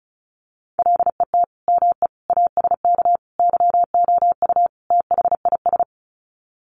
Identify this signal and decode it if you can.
Morse code, heard as a keyed tone, about 34 wpm, 700 Hz. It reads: LET ME ASK YOU THIS
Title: Sets of 5 Words 35wpm